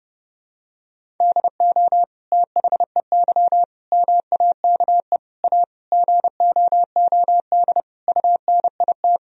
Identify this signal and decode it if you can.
Morse code, heard as a keyed tone, about 30 wpm, 700 Hz. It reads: DO THEY MAKE A GOOD UNIT
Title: Sentences from Top 1000 Words - Review Part 2 - 30wpm